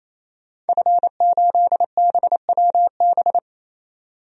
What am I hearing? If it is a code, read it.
Morse code, heard as a keyed tone, about 28 wpm, 700 Hz. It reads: F8BWB
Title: DXCC-100 Call Signs - 5 Characters - 28wpm